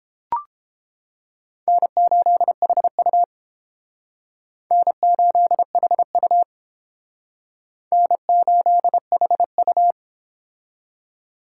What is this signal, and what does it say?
Morse code, heard as a keyed tone, about 30 wpm, 700 Hz. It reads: N8HU N8HU N8HU